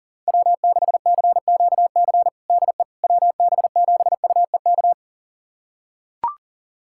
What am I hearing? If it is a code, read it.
Morse code, heard as a keyed tone, about 40 wpm, 700 Hz. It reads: W6CQC DE WB7UEK
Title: USA Callsign DE Callsign 40wpm